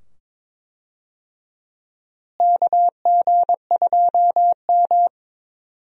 Morse code, heard as a keyed tone, about 21 wpm, 700 Hz. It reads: KG2M
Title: USA Call Signs - 4 Characters - Repeated 2x - 22wpm